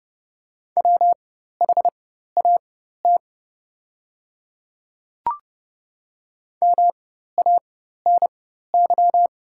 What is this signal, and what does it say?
Morse code, heard as a keyed tone, about 30 wpm, spaced as for 14 wpm, 700 Hz. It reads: WHAT MANY
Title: 4x Character Spacing - Top 100 Words - 30wpm